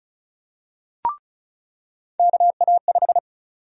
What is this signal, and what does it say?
Morse code, heard as a keyed tone, about 36 wpm, 700 Hz. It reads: KA5
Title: USA Call Sign Prefixes - 35wpm